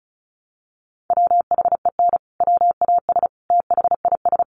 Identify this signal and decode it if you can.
Morse code, heard as a keyed tone, about 34 wpm, 700 Hz. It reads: WHEN WAS THIS
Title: Sentences from Top 1200 Words 35wpm